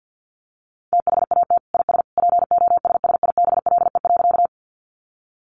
Morse code, heard as a keyed tone, about 50 wpm, 700 Hz. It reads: THAT IS POSSIBLE.
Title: Sentences from Top 500 Words 50wpm